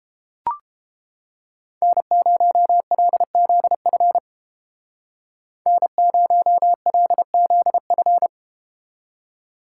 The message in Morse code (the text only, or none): N0LZF N0LZF